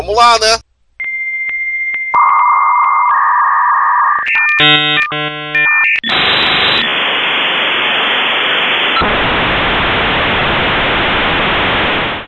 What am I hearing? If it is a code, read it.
Dial-up modem handshake, with speech.